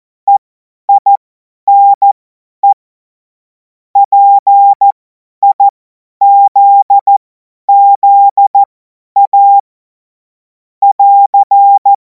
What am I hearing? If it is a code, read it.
Morse code, heard as a keyed tone, about 14 wpm, spaced as for 10 wpm, 800 Hz. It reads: EINE PIZZA <AR>
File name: Morsekurs / Witze / Witz 21. April 2026 50 F70